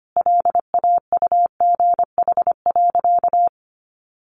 Morse code, heard as a keyed tone, about 25 wpm, 700 Hz. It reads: LAUGH.
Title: Sentences from Top 500 Words 25wpm